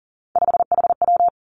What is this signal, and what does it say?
Morse code, heard as a keyed tone, about 41 wpm, 700 Hz. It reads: 5HW
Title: Three Challenging Letter-Number Combinations 40wpm